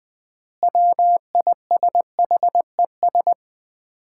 Morse code, heard as a keyed tone, about 20 wpm, 700 Hz. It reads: WISHES
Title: Common QSO Phrases - 3x Word-Spacing - 20wpm